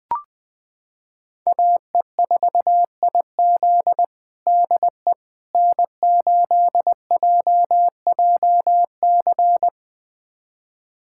Morse code, heard as a keyed tone, about 20 wpm, 700 Hz. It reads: AE4IZ DE N8JJC
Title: USA Callsign DE Callsign - 5 Characters - 20wpm